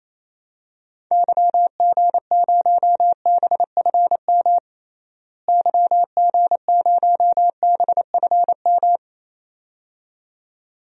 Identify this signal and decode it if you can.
Morse code, heard as a keyed tone, about 28 wpm, 700 Hz. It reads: YG0BFM YG0BFM
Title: DXCC-100 Call Signs - 6 Characters - Repeated 2x - 28wpm